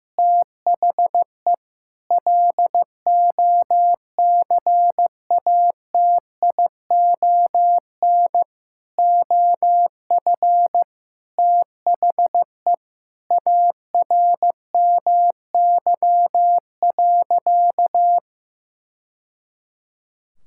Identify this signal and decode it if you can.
Morse code, heard as a keyed tone, about 15 wpm, 700 Hz. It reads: THE LOCATION OF THE ARMY.
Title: Sentences from Top 1000 Words - Review Part 6 - 15wpm